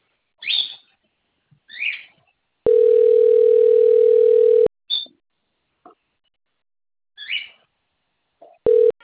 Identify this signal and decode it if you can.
Ringback tone (North America), 440 + 480 Hz.
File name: rg-1600-unknown- - - .16007